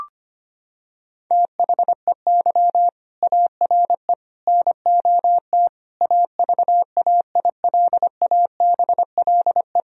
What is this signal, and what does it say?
Morse code, heard as a keyed tone, about 25 wpm, 700 Hz. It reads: THEY ARE NOT AVAILABLE
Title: Sentences from Top 1000 Words - Review Part 3 - 25wpm